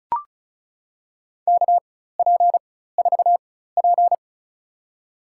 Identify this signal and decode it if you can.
Morse code, heard as a keyed tone, about 34 wpm, spaced as for 16 wpm, 700 Hz. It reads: KP4P
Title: USA Call Signs - 4x Character Spacing - 35wpm